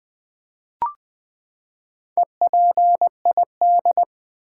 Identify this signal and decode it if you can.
Morse code, heard as a keyed tone, about 20 wpm, 700 Hz. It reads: EPID
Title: Build A Word - 4 Letters At A Time - 8-Letter Words - 20wpm